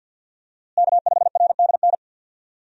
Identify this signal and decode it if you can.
Morse code, heard as a keyed tone, about 50 wpm, 700 Hz. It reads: K5RDN
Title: USA Call Signs - 5 Characters - Repeated 2x - 50wpm